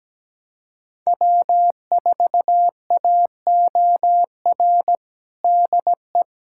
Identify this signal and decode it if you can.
Morse code, heard as a keyed tone, about 17 wpm, 700 Hz. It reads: W4AOR DE
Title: USA Callsign DE Callsign - 5 Characters - 17wpm